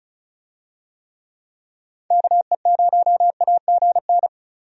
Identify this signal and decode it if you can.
Morse code, heard as a keyed tone, about 34 wpm, 700 Hz. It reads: KE0AGN
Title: USA Call Signs - 6 Characters - 35wpm